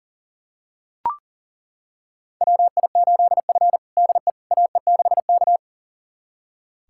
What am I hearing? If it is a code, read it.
Morse code, heard as a keyed tone, about 40 wpm, 700 Hz. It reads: WI8F DE AE6K
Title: USA Callsign DE Callsign - 4 Characters - 40wpm